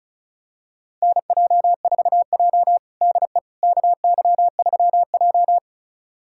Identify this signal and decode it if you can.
Morse code, heard as a keyed tone, about 34 wpm, 700 Hz. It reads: NJ4J DE KY3J
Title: USA Callsign DE Callsign - 4 Characters - 35wpm